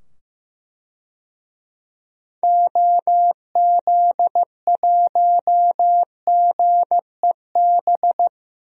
Morse code, heard as a keyed tone, about 15 wpm, 700 Hz. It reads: OZ1GEB
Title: DXCC-100 Call Signs - 6 Characters - 15wpm